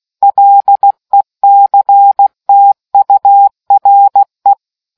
Morse code, heard as a keyed tone, about 16 wpm, 800 Hz. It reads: LECTURE